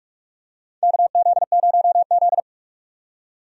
Morse code, heard as a keyed tone, about 46 wpm, 700 Hz. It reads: KZ0Z
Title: USA Call Signs - 4 Characters - 45wpm